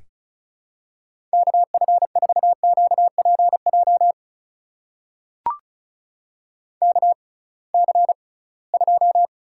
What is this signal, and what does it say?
Morse code, heard as a keyed tone, about 36 wpm, spaced as for 23 wpm, 700 Hz. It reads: KF4QPJ KC2
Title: USA Call Signs - 6 Characters - 6x Character Spacing - 35wpm